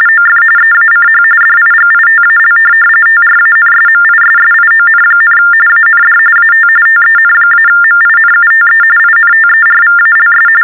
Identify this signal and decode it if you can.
TTY/TDD text telephone tones, about 45.45 baud. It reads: OF EACH DIGITAL TRANSMISSION, AND AT TEN-MINUTE INTERVALS
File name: teletype-RTTY-45